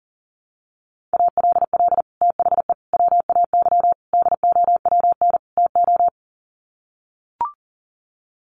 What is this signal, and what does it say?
Morse code, heard as a keyed tone, about 37 wpm, 700 Hz. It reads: ALL THE WAY DOWN TO
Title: Sets of 5 Words Encore2 40wpm